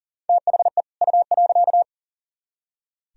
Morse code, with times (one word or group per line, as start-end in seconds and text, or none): 0.3-0.8 THE
1.0-1.8 U.